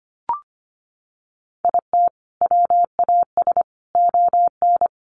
Morse code, heard as a keyed tone, about 25 wpm, 700 Hz. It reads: IT WAS ON
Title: Sentences from Top 800 Words 25wpm